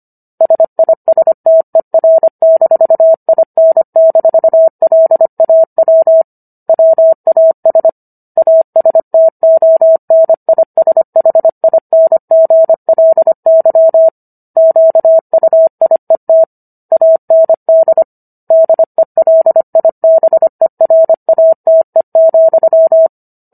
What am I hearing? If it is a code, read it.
Morse code, heard as a keyed tone, about 25 wpm, 650 Hz. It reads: SISTER-IN-LAW WAS ASTONISHINGLY QUIET AND DELIBERATE,